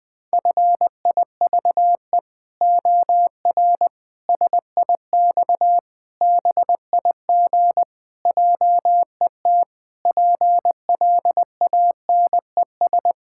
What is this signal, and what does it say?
Morse code, heard as a keyed tone, about 21 wpm, 700 Hz. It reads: FIVE OR SIX BIG JET PLANES
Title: Pangram Challenge v2 - 20wpm